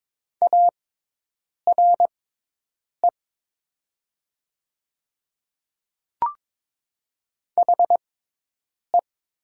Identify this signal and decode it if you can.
Morse code, heard as a keyed tone, about 22 wpm, spaced as for 8 wpm, 700 Hz. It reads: ARE HE
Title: 6x Character Spacing - Top 100 Words - 22wpm